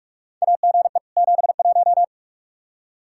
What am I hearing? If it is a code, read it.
Morse code, heard as a keyed tone, about 46 wpm, 700 Hz. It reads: AGE 71
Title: Practice QSO - Exchange 3 - More Info - 45wpm